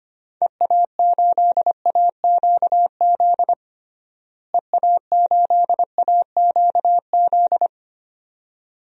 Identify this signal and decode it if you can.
Morse code, heard as a keyed tone, about 25 wpm, 700 Hz. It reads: EA8AQZ EA8AQZ
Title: DXCC-100 Call Signs - 6 Characters - Repeated 2x - 25wpm